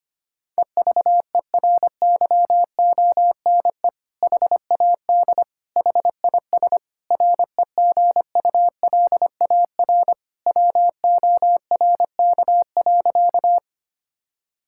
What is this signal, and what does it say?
Morse code, heard as a keyed tone, about 25 wpm, 700 Hz. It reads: EVERYONE HAD HIS REGULAR WORK.
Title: Sentences from Top 1000 Words - Review Part 6 - 25wpm